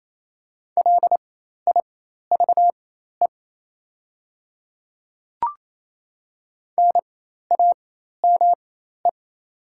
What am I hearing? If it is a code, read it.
Morse code, heard as a keyed tone, about 28 wpm, spaced as for 13 wpm, 690 Hz. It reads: LIVE NAME
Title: 4x Character Spacing - Top 200 Words - 28wpm